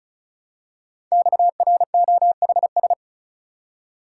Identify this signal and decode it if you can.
Morse code, heard as a keyed tone, about 36 wpm, 700 Hz. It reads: XROHS
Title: Code Groups - 5 Letters - 35wpm